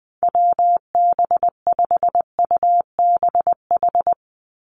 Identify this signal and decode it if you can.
Morse code, heard as a keyed tone, about 20 wpm, 700 Hz. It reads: WB5UBH
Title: USA Call Signs 20wpm